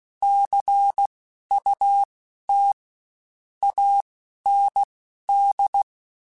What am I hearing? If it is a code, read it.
Morse code, heard as a keyed tone, about 16 wpm, spaced as for 11 wpm, 800 Hz. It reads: CUT AND